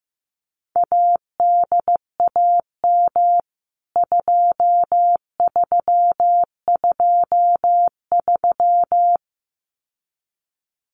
Morse code, heard as a keyed tone, about 15 wpm, 700 Hz. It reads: ADAM 2323
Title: CWT Exchange 15wpm